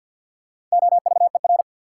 Morse code, heard as a keyed tone, about 52 wpm, 700 Hz. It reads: OVER
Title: Sentences from Top 100 Words - New Word Drill - 50wpm